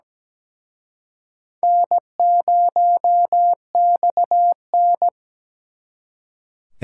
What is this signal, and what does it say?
Morse code, heard as a keyed tone, about 20 wpm, 700 Hz. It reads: N0XN
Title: Speed Racing - USA 1x2 Call Signs 17wpm